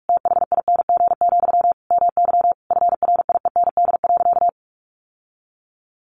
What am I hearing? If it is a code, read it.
Morse code, heard as a keyed tone, about 44 wpm, 700 Hz. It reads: THING, MY FRIEND.
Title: Sentences from Top 1200 Words 45wpm